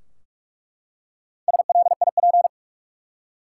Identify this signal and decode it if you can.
Morse code, heard as a keyed tone, about 46 wpm, 700 Hz. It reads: SLIP
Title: Four Letter Words - 45wpm